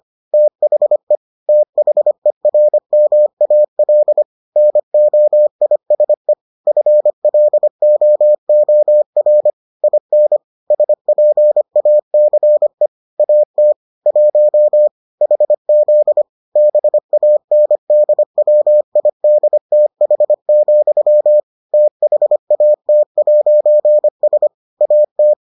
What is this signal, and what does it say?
Morse code, heard as a keyed tone, about 25 wpm, 600 Hz. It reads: THE THERMAL NOISE FLOOR IN SPACE AT 1 HZ BANDWIDTH, THAT'S AT